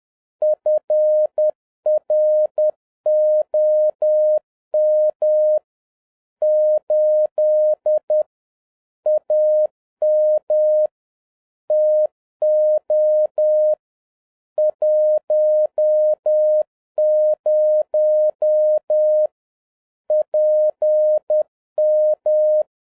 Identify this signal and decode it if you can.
Morse code, heard as a keyed tone, about 10 wpm, 610 Hz. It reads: FROM 8 AM TO 10 PM
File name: 2011-07-mp3 / n56 BACK FROM A BIG ACCIDENT 10wpm